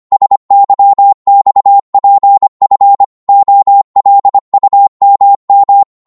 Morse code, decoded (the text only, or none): SYXPF OLUMM